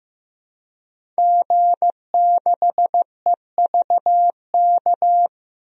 Morse code, heard as a keyed tone, about 15 wpm, 700 Hz. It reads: G6EVK
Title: DXCC-100 Call Signs - 5 Characters - 15wpm